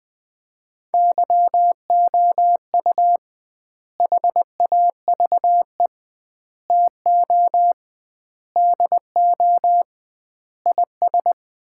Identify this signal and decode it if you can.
Morse code, heard as a keyed tone, about 20 wpm, 700 Hz. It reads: YOU HAVE TO DO IS
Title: 2x Word-Spacing - Sets of 5 Words - 20wpm